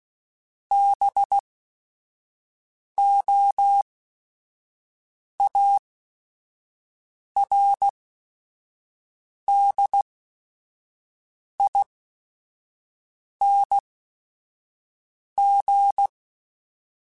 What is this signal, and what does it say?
Morse code, heard as a keyed tone, about 16 wpm, spaced as for 5 wpm, 800 Hz. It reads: BOARDING
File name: Special INGs / Special INGs-mp3s / Sngl Ing @5